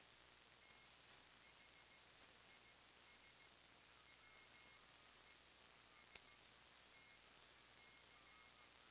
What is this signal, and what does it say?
Morse code, heard as a keyed tone, about 15 wpm, spaced as for 10 wpm, 2150 Hz, with noise. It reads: THIS WESTC
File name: sdr / WINRAD WB5RVZ PSK Sample 40m